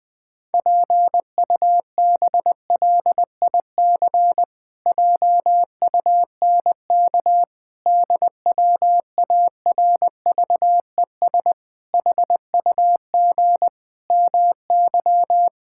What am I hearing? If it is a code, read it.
Morse code, heard as a keyed tone, about 21 wpm, 700 Hz. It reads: PUBLIC JUNK DWARVES HUG MY
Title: Pangram Challenge v2 - 20wpm